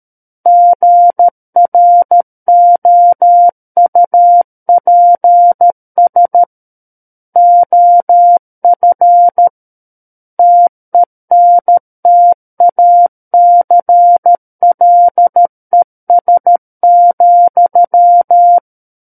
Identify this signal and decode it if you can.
Morse code, heard as a keyed tone, about 13 wpm, 700 Hz. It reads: GROUPS OF TENTACLES,
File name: Book One / 04-Chapter 4 - 13 WPM